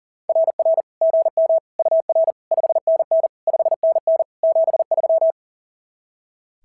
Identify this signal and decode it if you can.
Morse code, heard as a keyed tone, about 40 wpm, 660 Hz. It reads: RR GM UR 5NN 5NN 73
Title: SOTA QSO Practice 40wpm